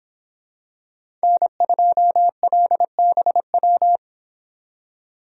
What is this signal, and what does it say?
Morse code, heard as a keyed tone, about 25 wpm, 700 Hz. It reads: N2LBW